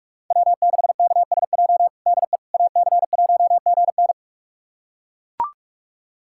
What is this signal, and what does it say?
Morse code, heard as a keyed tone, about 41 wpm, 700 Hz. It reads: W6KSJ DE AC1GN
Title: USA Callsign DE Callsign - 5 Characters - 45wpm